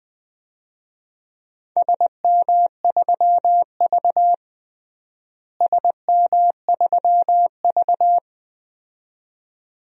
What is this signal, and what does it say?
Morse code, heard as a keyed tone, about 20 wpm, 700 Hz. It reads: SM3V SM3V
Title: DXCC-100 Call Signs - 4 Characters - Repeated 2x - 20wpm